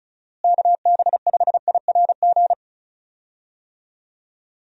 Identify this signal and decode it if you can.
Morse code, heard as a keyed tone, about 36 wpm, 700 Hz. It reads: KB5IRG
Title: USA Call Signs - 6 Characters - Repeated 3x - 35wpm